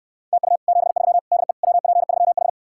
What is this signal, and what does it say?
Morse code, heard as a keyed tone, about 69 wpm, 700 Hz. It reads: TU 73 DE WP3H